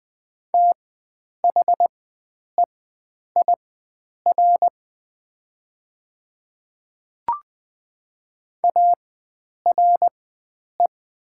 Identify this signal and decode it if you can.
Morse code, heard as a keyed tone, about 20 wpm, spaced as for 9 wpm, 700 Hz. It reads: THEIR ARE
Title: 4x Character Spacing - Top 100 Words - 20wpm